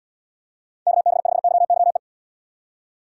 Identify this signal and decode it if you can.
Morse code, heard as a keyed tone, about 73 wpm, 700 Hz. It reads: KB5YYE